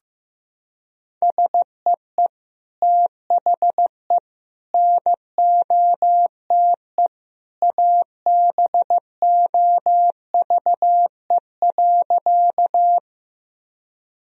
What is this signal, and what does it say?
Morse code, heard as a keyed tone, about 15 wpm, 700 Hz. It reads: SEE THE NOTE ABOVE.
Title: Sentences from Top 1000 Words - Review Part 2 - 15wpm